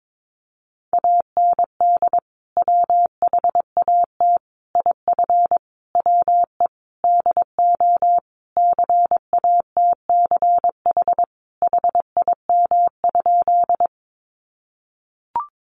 Morse code, heard as a keyed tone, about 22 wpm, 700 Hz. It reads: AND WHAT IF WE DO CATCH HIM?
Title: Sentences from Top 1500 Words 22wpm